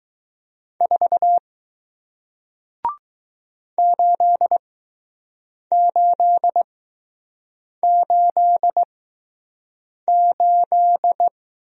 Morse code, heard as a keyed tone, about 20 wpm, spaced as for 6 wpm, 700 Hz. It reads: 4 8888